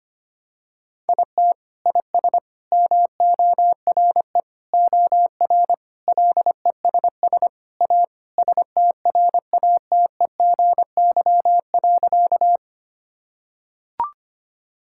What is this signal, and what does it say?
Morse code, heard as a keyed tone, about 25 wpm, 700 Hz. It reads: IT IS MORE OR LESS A STRATEGY.
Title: Sentences from Top 1000 Words - Review Part 5 - 25wpm